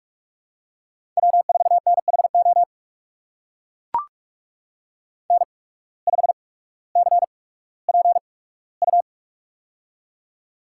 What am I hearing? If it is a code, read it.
Morse code, heard as a keyed tone, about 44 wpm, spaced as for 19 wpm, 700 Hz. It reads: W4NHO N5CPU
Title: USA Call Signs - 8x Character Spacing - 45wpm